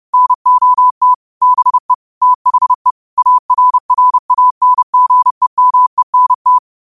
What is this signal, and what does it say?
Morse code, heard as a keyed tone, about 30 wpm, 1000 Hz. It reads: NOT BE THE ARRANGEMENT
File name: qotdfiles / 30wpm